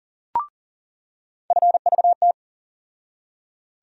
Morse code, heard as a keyed tone, about 40 wpm, 700 Hz. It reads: FVT
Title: USA Call Sign Postfixes - 3 Letters 40wpm